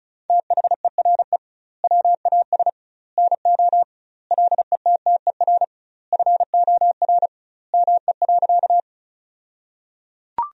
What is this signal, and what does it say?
Morse code, heard as a keyed tone, about 34 wpm, 700 Hz. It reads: THERE WAS NO LETTER FOR ME.
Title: 2x Word-Spacing - Sentences from Top 300 Words - 35wpm